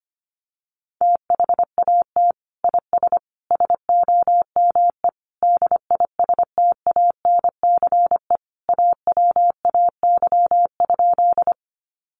Morse code, heard as a keyed tone, about 25 wpm, 700 Hz. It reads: THAT IS SOME DISTANCE AWAY?
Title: Sentences from Top 1200 Words 25wpm